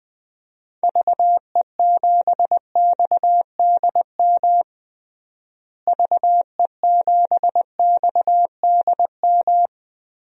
Morse code, heard as a keyed tone, about 21 wpm, 700 Hz. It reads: VE7XDM VE7XDM
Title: DXCC-100 Call Signs - 6 Characters - Repeated 3x - 20wpm